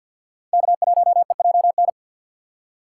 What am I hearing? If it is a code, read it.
Morse code, heard as a keyed tone, about 47 wpm, 700 Hz. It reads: K1EJN
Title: USA Call Signs - 5 Characters - 3x Character Spacing - 50wpm